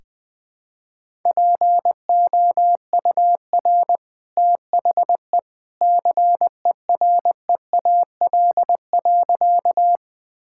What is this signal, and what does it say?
Morse code, heard as a keyed tone, about 20 wpm, 700 Hz. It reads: POUR THE CEREAL.
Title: Easy Sentences with Hints - 20wpm